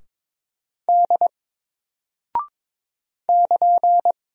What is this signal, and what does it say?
Morse code, heard as a keyed tone, about 22 wpm, 700 Hz. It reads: D <KN>
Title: Single Character (Letter:Number:Punctuation:Prosign) 22wpm